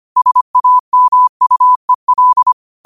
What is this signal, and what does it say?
Morse code, heard as a keyed tone, about 25 wpm, 1000 Hz. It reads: SAMUEL
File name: qotdfiles / 25wpm